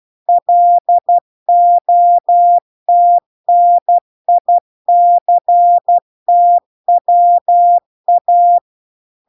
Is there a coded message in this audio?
Yes: Morse code from 0.3 to 8.6 s, heard as a keyed tone, about 12 wpm, 700 Hz. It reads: LOTNICTWA